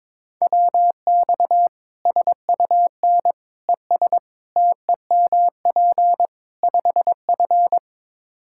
Morse code, heard as a keyed tone, about 22 wpm, 700 Hz. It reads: WX SUN ES TEMP 5F
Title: Practice QSO - Exchange 2 - Info - 22wpm